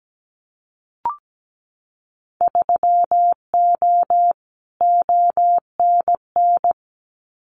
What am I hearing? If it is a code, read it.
Morse code, heard as a keyed tone, about 17 wpm, 700 Hz. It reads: 3O ONN
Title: Winter Field Day Exchanges 17wpm